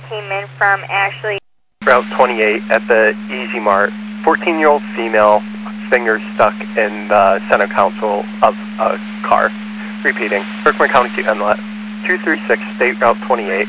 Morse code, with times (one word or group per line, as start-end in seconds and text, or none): none